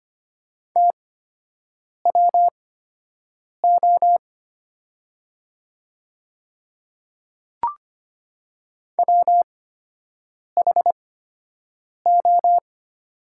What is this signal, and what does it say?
Morse code, heard as a keyed tone, about 25 wpm, spaced as for 7 wpm, 700 Hz. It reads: TWO WHO
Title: 8x Character Spacing - Top 200 Words - 25wpm